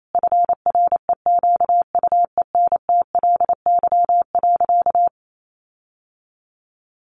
Morse code, heard as a keyed tone, about 28 wpm, 700 Hz. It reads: FREQUENTLY.
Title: Sentences from Top 1300 Words 28wpm